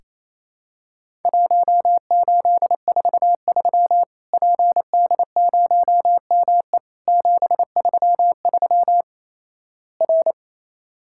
Morse code, heard as a keyed tone, about 28 wpm, 700 Hz. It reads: 1843 PD0ME 733 R
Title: Worked All Europe DX Contest - QTC Exchange - 28wpm